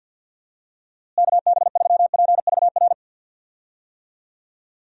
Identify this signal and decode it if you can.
Morse code, heard as a keyed tone, about 50 wpm, 700 Hz. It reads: KB3PVR
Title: USA Call Signs - 6 Characters - Repeated 2x - 50wpm